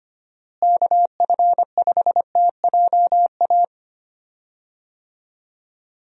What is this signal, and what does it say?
Morse code, heard as a keyed tone, about 25 wpm, 700 Hz. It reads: KF5TJA
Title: USA Call Signs - 6 Characters - Repeated 2x - 25wpm